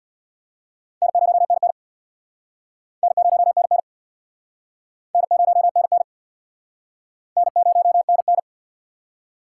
Morse code, heard as a keyed tone, about 64 wpm, 700 Hz. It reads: N0NN N0NN N0NN N0NN